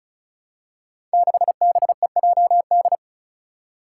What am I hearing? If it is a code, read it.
Morse code, heard as a keyed tone, about 36 wpm, 700 Hz. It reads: 6BEJD
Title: Five Challenging Letter-Number Combinations 35wpm